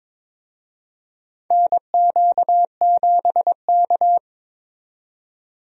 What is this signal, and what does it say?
Morse code, heard as a keyed tone, about 22 wpm, 700 Hz. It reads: NQ7K
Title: USA Call Signs - 4 Characters - 22wpm